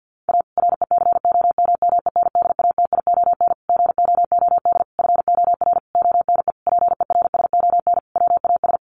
Morse code, heard as a keyed tone, about 50 wpm, 700 Hz. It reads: A RECOMMENDATION GOOD FOR ONE PERSON WAS
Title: Sentences from Top 1400 Words 50wpm